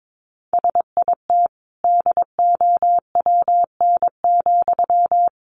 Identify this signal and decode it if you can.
Morse code, heard as a keyed tone, about 22 wpm, 700 Hz. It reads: SIT DOWN,
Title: Sentences from Top 500 Words 22wpm